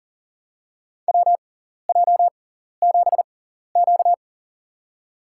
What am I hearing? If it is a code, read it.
Morse code, heard as a keyed tone, about 42 wpm, spaced as for 14 wpm, 700 Hz. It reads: WJ7Q
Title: USA Call Signs - 4 Characters - 6x Character Spacing - 40wpm